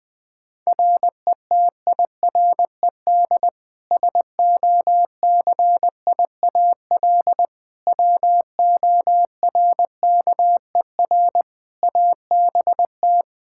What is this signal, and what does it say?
Morse code, heard as a keyed tone, about 20 wpm, 700 Hz. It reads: RETIRED SOCIAL WORKER ABT
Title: K1USN SST QSO Practice 20wpm